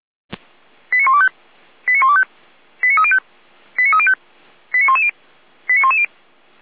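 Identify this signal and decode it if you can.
Five-tone selective call (ZVEI), heard as quick runs of five tones, tones of about 70 ms. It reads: 89125 89125 89305 89305 89110 89110
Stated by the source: ZVEI five-tone selective call 89140, SDR recording, Feuerwehr Orlamünde.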